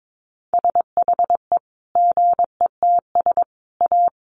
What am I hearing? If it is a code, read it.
Morse code, heard as a keyed tone, about 22 wpm, 700 Hz. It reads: SHE GETS A
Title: Easy Phrases 22wpm